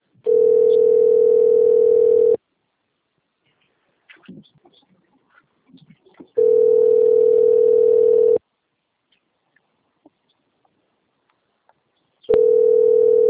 Ringback tone (North America), 440 + 480 Hz.
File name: rg-1600-unknown- - - .14934